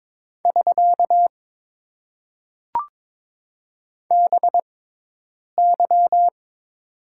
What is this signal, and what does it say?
Morse code, heard as a keyed tone, about 21 wpm, spaced as for 8 wpm, 700 Hz. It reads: <SK> BY
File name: Top 90 QSO Elements - 6x Character Spacing - 22wpm